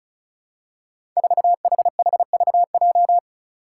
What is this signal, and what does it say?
Morse code, heard as a keyed tone, about 36 wpm, 700 Hz. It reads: 4HHVJ
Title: Five Challenging Letter-Number Combinations 35wpm